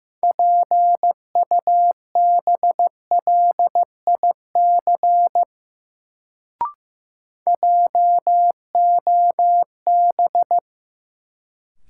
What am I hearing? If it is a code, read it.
Morse code, heard as a keyed tone, about 15 wpm, 700 Hz. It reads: PUBLIC JOB
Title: Sentences from Top 600 Words - New Word Drill - 15wpm